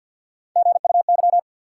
Morse code, heard as a keyed tone, about 48 wpm, 700 Hz. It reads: GUY
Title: Three Letter Words - 50wpm